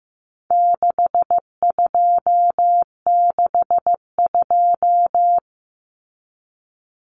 Morse code, heard as a keyed tone, about 15 wpm, 700 Hz. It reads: 6262
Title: Numbers - 4-Digit 15wpm